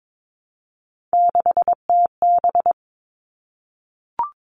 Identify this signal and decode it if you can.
Morse code, heard as a keyed tone, about 22 wpm, 700 Hz. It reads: 6TB
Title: All Cut Numbers - 3-Digit 22wpm